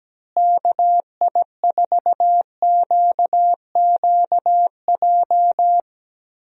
Morse code, heard as a keyed tone, about 17 wpm, 700 Hz. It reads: KI4QQJ
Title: USA Call Signs - 6 Characters - Repeated 3x - 17wpm